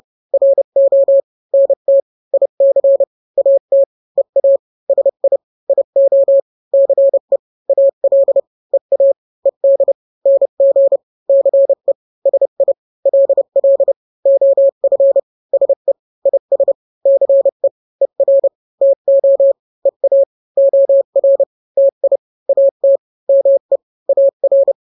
Morse code, heard as a keyed tone, about 30 wpm, 550 Hz. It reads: RO NT IC AT EA SI IO CE AL EA ED NG CE SI LL OF SE IS CE ER TO EA OR TI AT ME AR
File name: n-grams / 2-grams 30 25 WPM